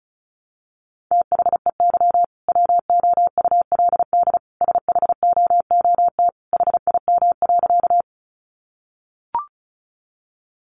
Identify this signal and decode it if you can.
Morse code, heard as a keyed tone, about 34 wpm, 700 Hz. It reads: THEY WOULD SHOOT HIM.
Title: Sentences from Top 1200 Words 35wpm